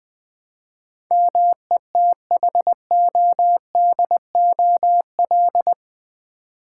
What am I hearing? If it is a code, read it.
Morse code, heard as a keyed tone, about 20 wpm, 700 Hz. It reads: METHODOL
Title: Build A Word - 4 Letters At A Time - 11-Letter Words - 20wpm